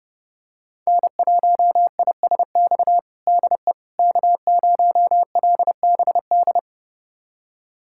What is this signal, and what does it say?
Morse code, heard as a keyed tone, about 29 wpm, 700 Hz. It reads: N1ISX DE K0LBD
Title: USA Callsign DE Callsign - 5 Characters - 30wpm